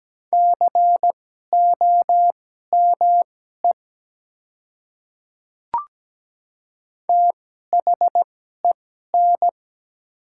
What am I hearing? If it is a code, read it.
Morse code, heard as a keyed tone, about 17 wpm, spaced as for 12 wpm, 700 Hz. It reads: COME THEN
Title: 2x Character Spacing - Top 100 Words - 17wpm